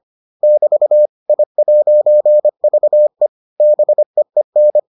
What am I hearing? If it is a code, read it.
Morse code, heard as a keyed tone, about 25 wpm, 600 Hz. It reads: <BT> I'VE BEEN